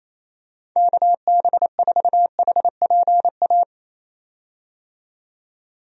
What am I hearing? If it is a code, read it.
Morse code, heard as a keyed tone, about 28 wpm, 700 Hz. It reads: KB4HPA
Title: USA Call Signs - 6 Characters - 28wpm